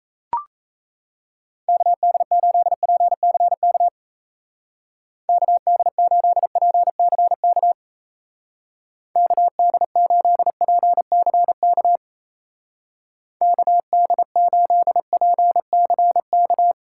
Morse code, heard as a keyed tone, about 34 wpm, 700 Hz. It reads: KD8PCK KD8PCK KD8PCK KD8PCK
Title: Speed Racing - USA 2x3 Call Signs 28wpm